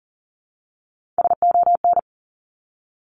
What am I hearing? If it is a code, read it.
Morse code, heard as a keyed tone, about 41 wpm, 700 Hz. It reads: SON
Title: Top Three Letter Combinations in Words 40wpm